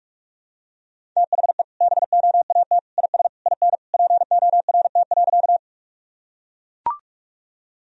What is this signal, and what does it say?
Morse code, heard as a keyed tone, about 42 wpm, 700 Hz. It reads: THE BOAT IS IN PORT.
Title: Sentences from Top 1000 Words - Review Part 1 - 45wpm